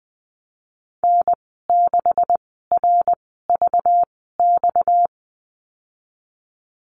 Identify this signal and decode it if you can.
Morse code, heard as a keyed tone, about 20 wpm, 700 Hz. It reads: N 6 R V X
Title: 231 Lesson 37g - Letter x - Call Signs 20wpm